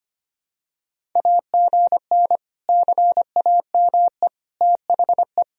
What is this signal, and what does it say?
Morse code, heard as a keyed tone, about 25 wpm, 700 Hz. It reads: AGN CAME THE
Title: Sentences from Top 500 Words and Abbreviations 25wpm